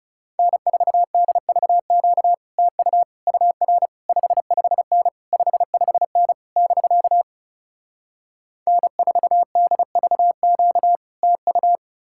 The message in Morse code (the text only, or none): N4DVQ TU UR 55N 55N <BK> N4DVQ TU